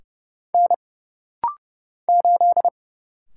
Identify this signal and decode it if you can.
Morse code, heard as a keyed tone, about 30 wpm, 700 Hz. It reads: N 8
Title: Single Letter-Number - Rapid-Fire 30wpm